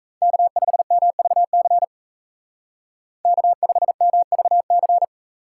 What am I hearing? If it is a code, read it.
Morse code, heard as a keyed tone, about 33 wpm, 700 Hz. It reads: K5MVC K5MVC
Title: Speed Racing - USA Call Signs 28wpm